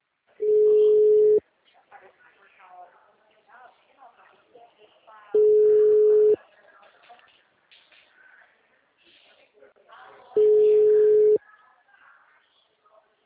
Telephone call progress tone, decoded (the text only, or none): ringback tone